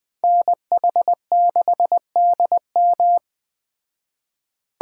Morse code, heard as a keyed tone, about 20 wpm, 700 Hz. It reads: NH6DM
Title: Speed Racing - USA 2x2 Call Signs 17wpm